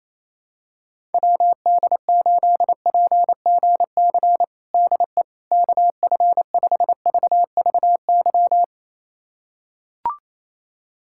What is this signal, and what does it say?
Morse code, heard as a keyed tone, about 28 wpm, 700 Hz. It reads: WD8PGC DE KF5VVY
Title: USA Callsign DE Callsign - 6 Characters - 28wpm